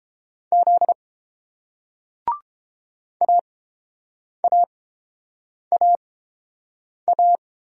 Morse code, heard as a keyed tone, about 30 wpm, spaced as for 7 wpm, 700 Hz. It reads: Z AAAA